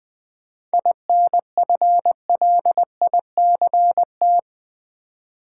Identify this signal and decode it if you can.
Morse code, heard as a keyed tone, about 20 wpm, 690 Hz. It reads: INFLICT
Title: 7 Letter Words - First 3 Letters Given - 20wpm